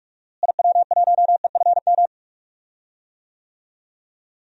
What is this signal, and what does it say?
Morse code, heard as a keyed tone, about 42 wpm, 700 Hz. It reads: IW1EFM
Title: DXCC-100 Call Signs - 6 Characters - Repeated 2x - 45wpm